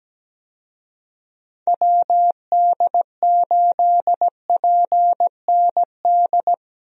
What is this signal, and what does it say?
Morse code, heard as a keyed tone, about 17 wpm, 700 Hz. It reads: WD8PND
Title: USA Call Signs - 6 Characters - 17wpm